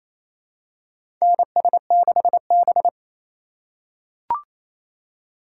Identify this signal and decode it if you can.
Morse code, heard as a keyed tone, about 28 wpm, 700 Hz. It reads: NS6B
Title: USA Call Signs - 4 Characters - 4x Character Spacing - 28wpm